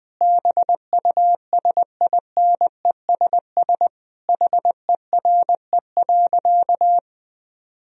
Morse code, heard as a keyed tone, about 20 wpm, 700 Hz. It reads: BUSINESS HERE.
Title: Sentences from Top 1000 Words - Review Part 4 - 20wpm